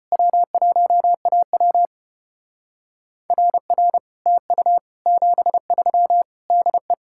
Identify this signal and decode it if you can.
Morse code, heard as a keyed tone, about 32 wpm, 700 Hz. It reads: W1AW RR TU 73 DE